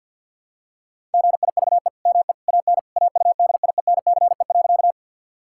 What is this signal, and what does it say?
Morse code, heard as a keyed tone, about 50 wpm, 700 Hz. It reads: GIVE ME AN AUDIENCE.
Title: Sentences from Top 1000 Words - Review Part 5 - 50wpm